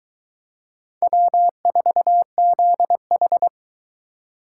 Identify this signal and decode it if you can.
Morse code, heard as a keyed tone, about 23 wpm, 700 Hz. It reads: W4ZH